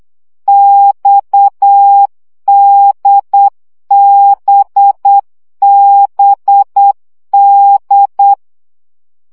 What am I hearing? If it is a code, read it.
Morse code, heard as a keyed tone, about 8 wpm, 800 Hz. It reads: XDBBD
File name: CW gyakorlas d b x 7 baud